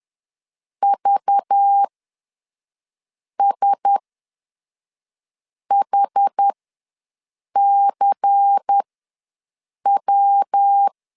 Morse code, heard as a keyed tone, about 10 wpm, spaced as for 5 wpm, 800 Hz. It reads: VSHCW
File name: LEZIONI / 9 2 ws 10wpm 4130